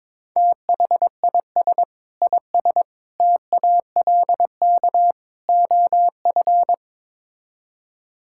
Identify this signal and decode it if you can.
Morse code, heard as a keyed tone, about 21 wpm, 700 Hz. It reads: THIS IS TALK OF
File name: practice / 22wpm / Build A Phrase - 5-Word Phrases - 22wpm